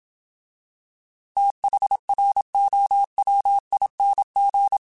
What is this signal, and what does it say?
Morse code, heard as a keyed tone, about 27 wpm, 800 Hz. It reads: THROWING
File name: TH / TH mp3s / Fast TH @25